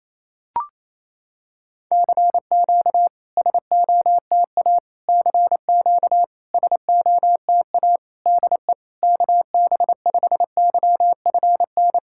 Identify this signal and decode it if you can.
Morse code, heard as a keyed tone, about 27 wpm, 700 Hz. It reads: CQ SOTA CQ SOTA DE KB5YFN
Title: SOTA QSO Practice 28wpm